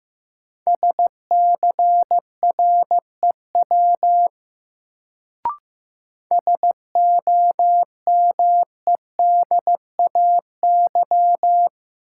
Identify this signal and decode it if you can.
Morse code, heard as a keyed tone, about 15 wpm, 700 Hz. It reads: SCREW SOMEDAY
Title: Sentences from Top 1600 Words - New Word Drill - 15wpm